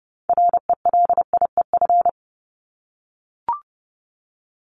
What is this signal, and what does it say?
Morse code, heard as a keyed tone, about 30 wpm, 700 Hz. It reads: RELIEF
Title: Top 1501-1750 Words 30wpm